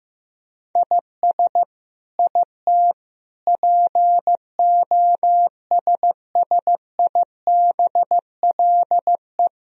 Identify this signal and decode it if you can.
Morse code, heard as a keyed tone, about 15 wpm, 700 Hz. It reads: IS IT POSSIBLE
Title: Sentences from Top 1000 Words - Review Part 4 - 15wpm